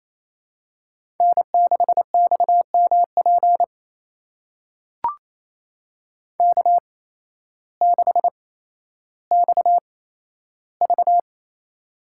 Morse code, heard as a keyed tone, about 28 wpm, spaced as for 13 wpm, 700 Hz. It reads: N6XMP K6XV
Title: USA Call Signs - 8x Character Spacing - 28wpm